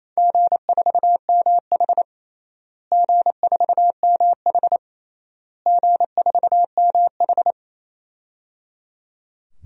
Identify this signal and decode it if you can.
Morse code, heard as a keyed tone, about 27 wpm, 700 Hz. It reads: G4MH G4MH G4MH
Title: DXCC-100 Call Signs - 4 Characters - Repeated 3x - 28wpm